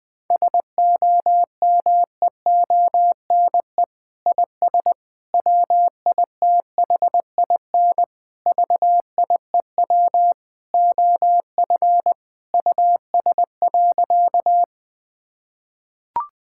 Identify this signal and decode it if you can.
Morse code, heard as a keyed tone, about 20 wpm, 700 Hz. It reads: SOMEONE IS WITHIN VIEW OF US.
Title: Sentences from Top 1000 Words - Review Part 6 - 20wpm